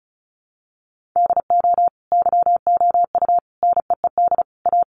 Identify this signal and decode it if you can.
Morse code, heard as a keyed tone, about 34 wpm, 700 Hz. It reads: DO YOU NEED A
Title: Sentences from Top 1500 Words 35wpm